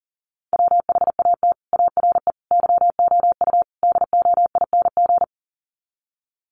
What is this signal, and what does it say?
Morse code, heard as a keyed tone, about 40 wpm, 700 Hz. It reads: WHAT ARE YOU DOING
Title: Sets of 4 Words 40wpm